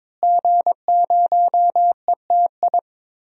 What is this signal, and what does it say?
Morse code, heard as a keyed tone, about 22 wpm, 700 Hz. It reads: G0ETI
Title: DXCC-100 Call Signs - 5 Characters - Repeated 3x - 22wpm